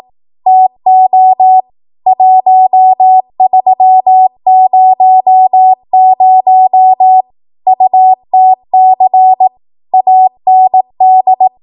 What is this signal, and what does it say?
Morse code, heard as a keyed tone, about 18 wpm, 750 Hz. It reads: TO 1300 UTC AND
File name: Archive / 18 WPM / 18WPM